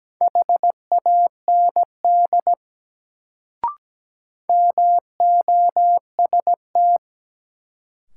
Morse code, heard as a keyed tone, about 17 wpm, 700 Hz. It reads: HAND MOST
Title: Sentences from Top 200 Words - New Word Drill - 17wpm